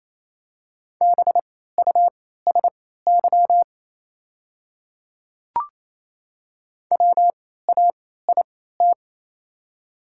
Morse code, heard as a keyed tone, about 28 wpm, spaced as for 16 wpm, 700 Hz. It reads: BUSY WAIT
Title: 3x Character Spacing - Top 500 Words - 28wpm